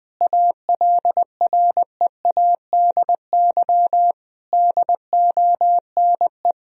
Morse code, heard as a keyed tone, about 20 wpm, 700 Hz. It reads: ALREADY DONE
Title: Sentences from Top 1000 Words - Review Part 2 - 20wpm